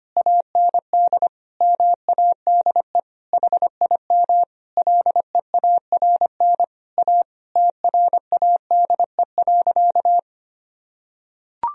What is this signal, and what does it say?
Morse code, heard as a keyed tone, about 25 wpm, 700 Hz. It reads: AND MADE HIM LEARN A TRADE.
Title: Sentences from Top 1000 Words - Review Part 4 - 25wpm